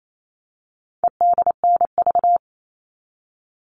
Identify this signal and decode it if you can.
Morse code, heard as a keyed tone, about 28 wpm, 700 Hz. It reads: EDNV
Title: All Cut Numbers - 4-Digit 28wpm